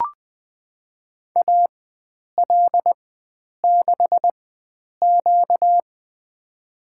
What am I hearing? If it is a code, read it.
Morse code, heard as a keyed tone, about 20 wpm, spaced as for 9 wpm, 700 Hz. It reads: AL6Q
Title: USA Call Signs - 4 Characters - 4x Character Spacing - 20wpm